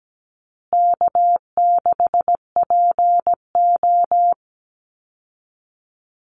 Morse code, heard as a keyed tone, about 17 wpm, 700 Hz. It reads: K6PO
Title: Call Signs Repeated 2x - 17wpm